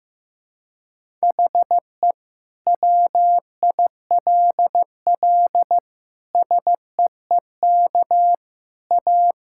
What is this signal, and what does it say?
Morse code, heard as a keyed tone, about 15 wpm, 700 Hz. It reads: HE WILL SEEK A
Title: Sentences from Top 1600 Words 15wpm